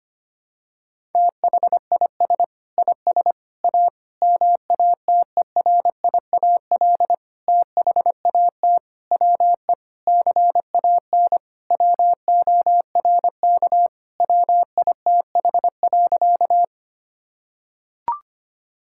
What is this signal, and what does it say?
Morse code, heard as a keyed tone, about 25 wpm, 700 Hz. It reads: THIS IS A MATERIAL THAT WE CAN WORK WITH.
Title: Sentences from Top 1000 Words - Review Part 5 - 25wpm